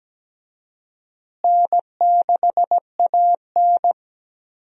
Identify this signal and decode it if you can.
Morse code, heard as a keyed tone, about 17 wpm, 700 Hz. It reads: N6AN
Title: USA Call Signs - 4 Characters - Repeated 2x - 17wpm